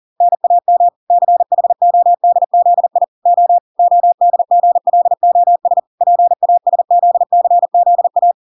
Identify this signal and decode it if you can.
Morse code, heard as a keyed tone, about 40 wpm, 700 Hz. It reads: NAM CHODZI O ODGLOS PASZCZA
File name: 40wpm